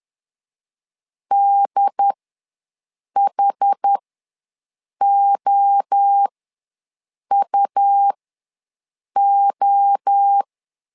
Morse code, heard as a keyed tone, about 10 wpm, spaced as for 6 wpm, 800 Hz. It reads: DHOUO
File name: LEZIONI / 3 MWRO / 3 MWRO ws 10wpm 2310